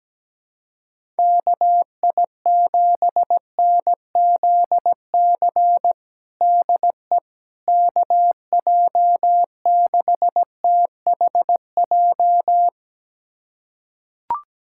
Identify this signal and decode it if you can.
Morse code, heard as a keyed tone, about 17 wpm, 700 Hz. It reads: KI7NZC DE KJ6THJ
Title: USA Callsign DE Callsign - 6 Characters - 17wpm